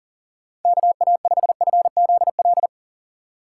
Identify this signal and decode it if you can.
Morse code, heard as a keyed tone, about 50 wpm, 700 Hz. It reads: KA5FZL
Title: Speed Racing - USA 2x3 Call Signs 40wpm